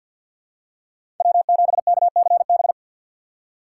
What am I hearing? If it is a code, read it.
Morse code, heard as a keyed tone, about 50 wpm, 700 Hz. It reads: W7KCB
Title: USA Call Signs - 3x Character Spacing - 50wpm